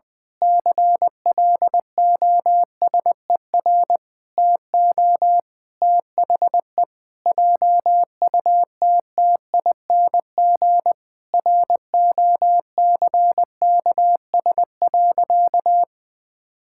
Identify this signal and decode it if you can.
Morse code, heard as a keyed tone, about 20 wpm, 700 Hz. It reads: CLOSER TO THE JUTTING ROCKS.
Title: Pangram Challenge v2 - 20wpm